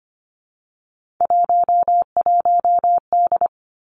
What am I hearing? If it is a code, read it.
Morse code, heard as a keyed tone, about 25 wpm, 700 Hz. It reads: 11D
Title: Three Challenging Letter-Number Combinations 25wpm